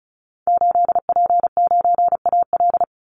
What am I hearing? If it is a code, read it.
Morse code, heard as a keyed tone, about 34 wpm, 700 Hz. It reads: 8P9AL
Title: Call Signs Repeated 3x - 35wpm